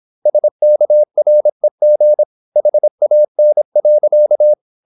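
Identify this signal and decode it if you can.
Morse code, heard as a keyed tone, about 26 wpm, 600 Hz. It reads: SKREG HAN.
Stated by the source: High-speed (QRQ) Morse at 26 wpm.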